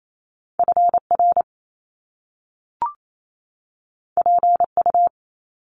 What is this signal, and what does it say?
Morse code, heard as a keyed tone, about 28 wpm, 700 Hz. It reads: FR PU
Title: Top Two Letter Combinations in Words 28wpm